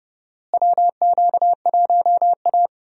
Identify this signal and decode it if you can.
Morse code, heard as a keyed tone, about 32 wpm, 700 Hz. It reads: WQ1A
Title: USA Call Signs - 4 Characters - Repeated 2x - 30wpm